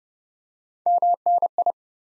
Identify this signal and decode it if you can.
Morse code, heard as a keyed tone, about 30 wpm, 700 Hz. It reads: MNI
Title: Top 90 Abbr 30WPM